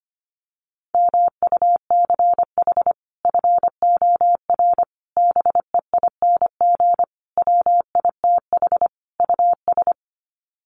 Morse code, heard as a keyed tone, about 25 wpm, 700 Hz. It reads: MUCH FOR BEING WITH US
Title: Sets of 5 Words 25wpm